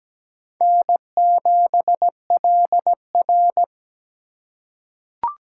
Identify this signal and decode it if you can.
Morse code, heard as a keyed tone, about 17 wpm, 700 Hz. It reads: N7LR
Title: K1USN SST QSO Practice - 17wpm